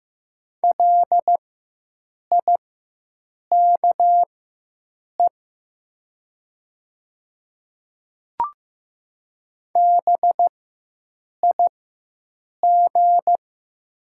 Morse code, heard as a keyed tone, about 15 wpm, spaced as for 7 wpm, 700 Hz. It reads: LIKE BIG
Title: 4x Character Spacing - Top 500 Words - 15wpm